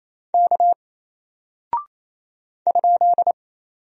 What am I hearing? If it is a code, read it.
Morse code, heard as a keyed tone, about 27 wpm, 700 Hz. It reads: K ?
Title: Single Character (Letter:Number:Punctuation:Prosign) 28wpm